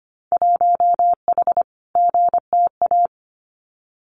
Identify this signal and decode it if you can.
Morse code, heard as a keyed tone, about 25 wpm, 700 Hz. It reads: 1H GTA
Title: Winter Field Day Exchanges 25wpm